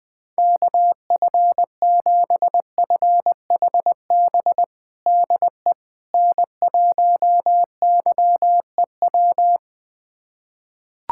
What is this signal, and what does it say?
Morse code, heard as a keyed tone, about 20 wpm, 700 Hz. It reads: KF7FHB DE N1YEW
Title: USA Callsign DE Callsign 20wpm